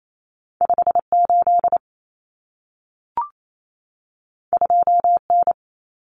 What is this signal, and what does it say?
Morse code, heard as a keyed tone, about 28 wpm, 700 Hz. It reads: 58 2N
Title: Common Cut Numbers - 2-Digit 28wpm